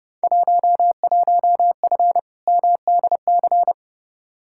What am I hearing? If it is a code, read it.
Morse code, heard as a keyed tone, about 30 wpm, 700 Hz. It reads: 11F MDC
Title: Field Day Exchanges Encore - 30wpm